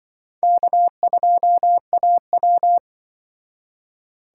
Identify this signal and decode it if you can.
Morse code, heard as a keyed tone, about 25 wpm, 700 Hz. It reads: K2AW
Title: Speed Racing - USA Call Signs 20wpm